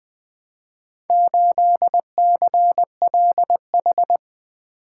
Morse code, heard as a keyed tone, about 20 wpm, 700 Hz. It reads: 8CLH
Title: Four Letter-Number Combinations - 20wpm